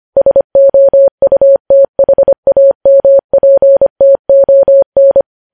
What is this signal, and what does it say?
Morse code, heard as a keyed tone, about 25 wpm, 550 Hz. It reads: SOUTHAMPTON
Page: News Headlines in Morse Code at 25 WPM